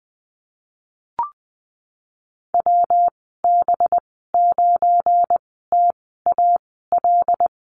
Morse code, heard as a keyed tone, about 20 wpm, spaced as for 15 wpm, 700 Hz. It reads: WB9TAL
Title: 193 Lesson 31g - Letter v - Call Signs 20wpm